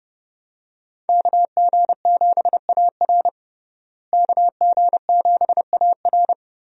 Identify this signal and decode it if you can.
Morse code, heard as a keyed tone, about 30 wpm, 700 Hz. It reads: KG7AR KG7AR
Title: USA Call Signs - 5 Characters - Repeated 3x - 30wpm